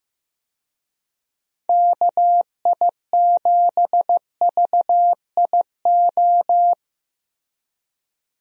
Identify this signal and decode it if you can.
Morse code, heard as a keyed tone, about 16 wpm, 700 Hz. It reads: KI7VIO
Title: USA Call Signs - 6 Characters - Repeated 3x - 15wpm